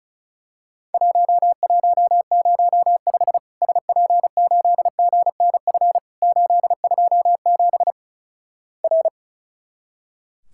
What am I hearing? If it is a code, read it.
Morse code, heard as a keyed tone, about 36 wpm, 700 Hz. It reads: 1105 SP8GNF 827 R
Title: Worked All Europe DX Contest - QTC Exchange - 35wpm